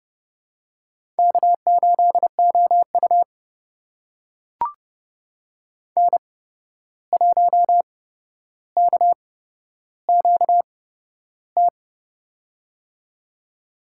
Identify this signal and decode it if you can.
Morse code, heard as a keyed tone, about 30 wpm, spaced as for 12 wpm, 700 Hz. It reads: K8OU N1KQT
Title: USA Call Signs - 8x Character Spacing - 30wpm